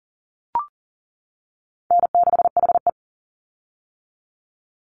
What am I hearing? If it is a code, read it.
Morse code, heard as a keyed tone, about 40 wpm, 700 Hz. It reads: N6HE
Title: Call Signs - 40wpm